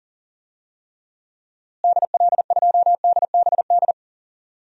Morse code, heard as a keyed tone, about 40 wpm, 700 Hz. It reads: DL2DBD
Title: DXCC-100 Call Signs - 6 Characters - Repeated 2x - 40wpm